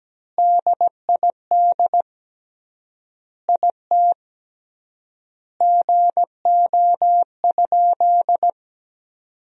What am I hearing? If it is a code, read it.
Morse code, heard as a keyed tone, about 17 wpm, 700 Hz. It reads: DID IT GO?
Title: 3x Word-Spacing - Sentences from Top 200 Words - 17wpm